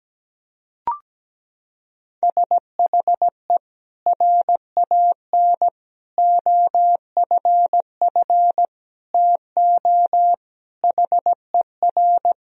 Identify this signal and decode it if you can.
Morse code, heard as a keyed tone, about 17 wpm, 700 Hz. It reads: SHE RAN OFF TO HER
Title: Sentences from Top 1000 Words - Review Part 3 - 17wpm